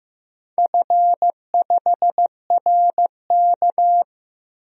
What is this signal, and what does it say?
Morse code, heard as a keyed tone, about 15 wpm, 700 Hz. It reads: F5RK
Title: DXCC-100 Call Signs - 4 Characters - 15wpm